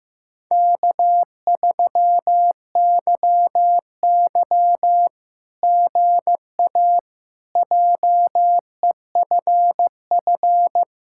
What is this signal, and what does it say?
Morse code, heard as a keyed tone, about 15 wpm, 700 Hz. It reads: K3YY GA JEFF
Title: SOTA QSO Practice 15wpm